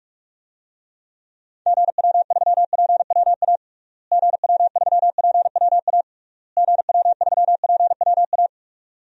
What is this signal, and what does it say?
Morse code, heard as a keyed tone, about 47 wpm, 700 Hz. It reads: GW3PWA GW3PWA GW3PWA
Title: DXCC-100 Call Signs - 6 Characters - Repeated 3x - 45wpm